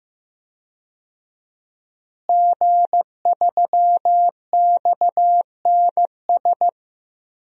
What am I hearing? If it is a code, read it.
Morse code, heard as a keyed tone, about 16 wpm, 700 Hz. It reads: G3XNS
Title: DXCC-100 Call Signs - 5 Characters - Repeated 3x - 15wpm